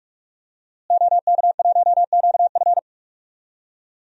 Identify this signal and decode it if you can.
Morse code, heard as a keyed tone, about 42 wpm, 700 Hz. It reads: OK1QF
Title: DXCC-100 Call Signs - 5 Characters - Repeated 2x - 45wpm